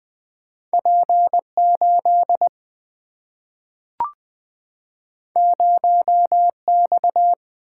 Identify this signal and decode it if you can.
Morse code, heard as a keyed tone, about 20 wpm, 700 Hz. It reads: P 8 0 X
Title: Single-Letter Single-Number 20wpm